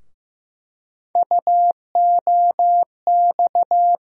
Morse code, heard as a keyed tone, about 15 wpm, 700 Hz. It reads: UOX
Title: USA Call Sign Postfixes - 3 Letters 15wpm